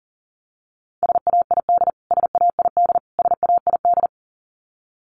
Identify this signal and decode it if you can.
Morse code, heard as a keyed tone, about 37 wpm, 700 Hz. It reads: SAID SAID SAID
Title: Top 300 Words Repeated 40wpm